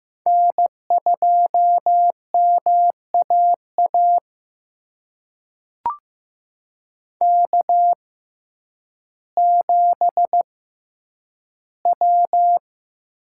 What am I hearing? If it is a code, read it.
Morse code, heard as a keyed tone, about 15 wpm, spaced as for 9 wpm, 700 Hz. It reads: N2MAA K7W
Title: USA Call Signs - 5 Characters - 6x Character Spacing - 15wpm